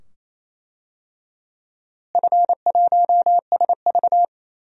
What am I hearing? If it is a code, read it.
Morse code, heard as a keyed tone, about 28 wpm, 700 Hz. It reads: F1SV
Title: DXCC-100 Call Signs - 4 Characters - Repeated 2x - 28wpm